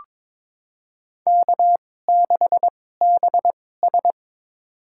Morse code, heard as a keyed tone, about 23 wpm, spaced as for 16 wpm, 700 Hz. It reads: K6BS
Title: USA Call Signs - 4 Characters - 2x Character Spacing - 22wpm